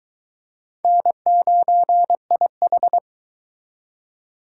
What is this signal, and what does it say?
Morse code, heard as a keyed tone, about 23 wpm, 700 Hz. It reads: N9IH